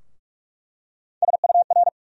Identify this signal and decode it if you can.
Morse code, heard as a keyed tone, about 42 wpm, 700 Hz. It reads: SUR
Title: USA Call Sign Postfixes - 3 Letters 45wpm